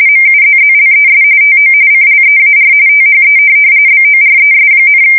RTTY, about 45.45 baud. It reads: MUZZLES TO CLEAR THE BORES,
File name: fsk170 / USS-New-Jersey